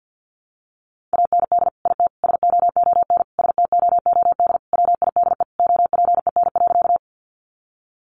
Morse code, heard as a keyed tone, about 50 wpm, 700 Hz. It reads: AND IT SOON STOOD WIDE OPEN.
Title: Sentences from Top 900 Words 50wpm